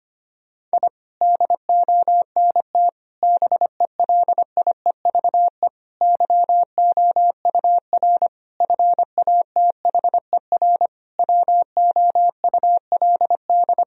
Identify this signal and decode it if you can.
Morse code, heard as a keyed tone, about 25 wpm, 700 Hz. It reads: I DONT BELIEVE YOUR FATHER WOULD
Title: Sentences from Top 1000 Words - Review Part 3 - 25wpm